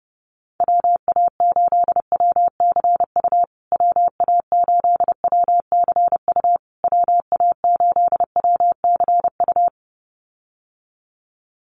Morse code, heard as a keyed tone, about 30 wpm, 700 Hz. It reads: WA8WCU WA8WCU WA8WCU
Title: USA Call Signs Repeated 3x 30wpm